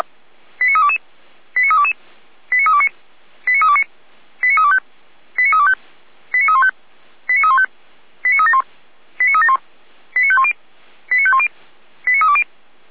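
Five-tone selective call (ZVEI), heard as quick runs of five tones, tones of about 70 ms. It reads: 89322 89322 89329 89329 89325 89325 89315 89315 89371 89371 89510 89510 89320
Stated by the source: ZVEI five-tone selective call received on SDR, audio.f85360.feuerwehr-orlamuende.org.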